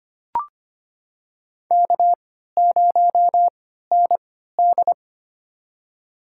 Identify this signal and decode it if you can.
Morse code, heard as a keyed tone, about 25 wpm, spaced as for 14 wpm, 700 Hz. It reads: K0ND
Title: USA Call Signs - 4 Characters - 3x Character Spacing - 25wpm